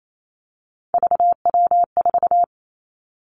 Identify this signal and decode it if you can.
Morse code, heard as a keyed tone, about 28 wpm, 700 Hz. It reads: VW4
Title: Three Challenging Letter-Number Combinations 28wpm